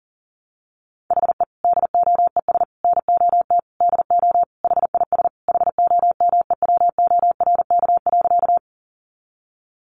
Morse code, heard as a keyed tone, about 37 wpm, 700 Hz. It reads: HE DOES NOT DO HIS HOMEWORK.
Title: Additional Easy Phrases 40wpm